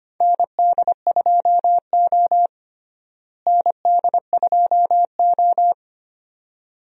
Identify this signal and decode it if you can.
Morse code, heard as a keyed tone, about 25 wpm, 700 Hz. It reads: ND2O ND2O
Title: USA Call Signs - 4 Characters - Repeated 2x - 25wpm